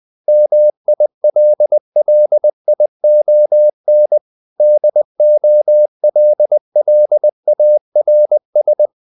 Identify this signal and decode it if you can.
Morse code, heard as a keyed tone, about 20 wpm, 600 Hz. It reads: MILLION DOLLARS
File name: CW / Practice / EnglishQuotations0320